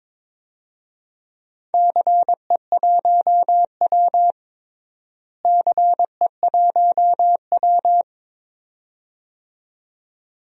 Morse code, heard as a keyed tone, about 22 wpm, 700 Hz. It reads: CE1W CE1W
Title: DXCC-100 Call Signs - 4 Characters - Repeated 2x - 22wpm